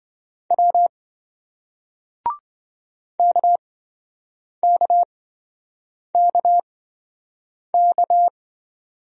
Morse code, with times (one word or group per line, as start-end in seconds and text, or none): 0.5-0.9 W
3.2-8.3 KKKK